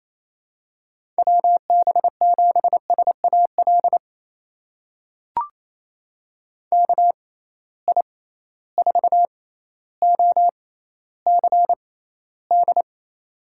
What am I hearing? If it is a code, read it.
Morse code, heard as a keyed tone, about 28 wpm, spaced as for 14 wpm, 700 Hz. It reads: WB7SAL KI4OCD
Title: USA Call Signs - 6x Character Spacing - 28wpm